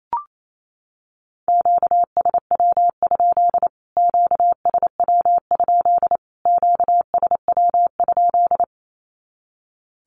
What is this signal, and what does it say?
Morse code, heard as a keyed tone, about 28 wpm, 700 Hz. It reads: QSW? QSW? QSW?
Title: Q-Codes Repeated 3x 28wpm